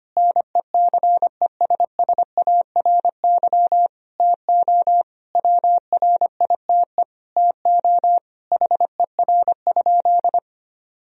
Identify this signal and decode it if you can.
Morse code, heard as a keyed tone, about 25 wpm, 700 Hz. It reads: NECESSARY TO WRITE TO HER?
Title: Sentences from Top 1000 Words - Review Part 1 - 25wpm